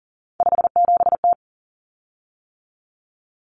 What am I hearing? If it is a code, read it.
Morse code, heard as a keyed tone, about 37 wpm, 700 Hz. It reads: 57T
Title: Common Cut Numbers - 3-Digit 40wpm